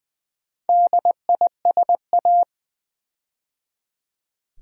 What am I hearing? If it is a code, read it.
Morse code, heard as a keyed tone, about 20 wpm, 700 Hz. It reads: DISA
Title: Build A Word - 4 Letters At A Time - 12-Letter Words - 20wpm